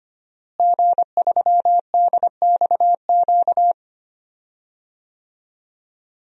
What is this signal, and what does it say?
Morse code, heard as a keyed tone, about 25 wpm, 700 Hz. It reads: G3DXQ
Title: DXCC-100 Call Signs - 5 Characters - Repeated 3x - 25wpm